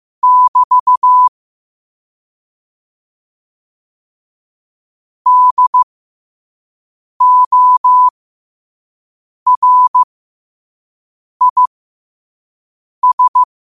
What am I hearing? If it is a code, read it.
Morse code, heard as a keyed tone, about 15 wpm, spaced as for 5 wpm, 1000 Hz. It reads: <BT> DORIS
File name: qotdfiles / 5wpm